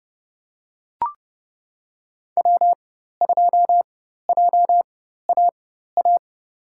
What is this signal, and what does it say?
Morse code, heard as a keyed tone, about 30 wpm, spaced as for 14 wpm, 700 Hz. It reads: W2JAA
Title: USA Call Signs - 5 Characters - 4x Character Spacing - 30wpm